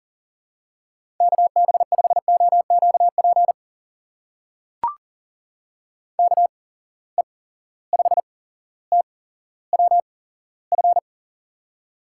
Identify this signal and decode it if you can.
Morse code, heard as a keyed tone, about 40 wpm, spaced as for 17 wpm, 700 Hz. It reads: KB5OQP KE5TWF
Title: USA Call Signs - 6 Characters - 8x Character Spacing - 40wpm